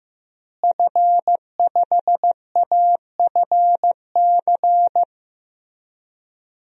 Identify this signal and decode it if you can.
Morse code, heard as a keyed tone, about 15 wpm, 700 Hz. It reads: F5AFC
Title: DXCC-100 Call Signs - 5 Characters - Repeated 3x - 15wpm